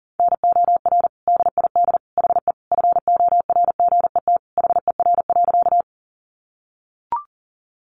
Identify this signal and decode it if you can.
Morse code, heard as a keyed tone, about 40 wpm, 700 Hz. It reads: NOR DID HE FORGET HER.
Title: Sentences from Top 1500 Words 40wpm